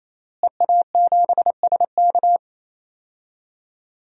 Morse code, heard as a keyed tone, about 28 wpm, 690 Hz. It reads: EA7SK
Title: DXCC-100 Call Signs - 5 Characters - Repeated 3x - 28wpm